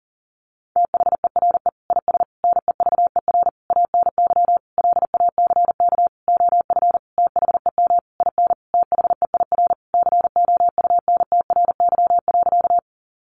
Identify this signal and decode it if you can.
Morse code, heard as a keyed tone, about 40 wpm, 700 Hz. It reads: THERE IS NEVER ANY LACK OF THEM IN THEIR COUNTRY.
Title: Sentences from Top 1100 Words 40wpm